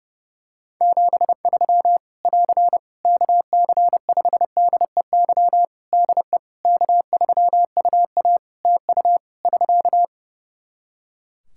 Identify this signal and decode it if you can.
Morse code, heard as a keyed tone, about 28 wpm, 700 Hz. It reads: 73 <AR> KC5DEY DE K3UA TU <SK>
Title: Practice QSO - Exchange 4 - Ending - 30wpm